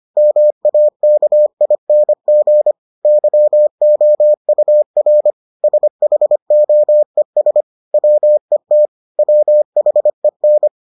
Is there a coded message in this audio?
Yes: Morse code from 0.2 to 10.7 s, heard as a keyed tone, about 25 wpm, 600 Hz. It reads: MAKING YOUR SHOES WET WHEN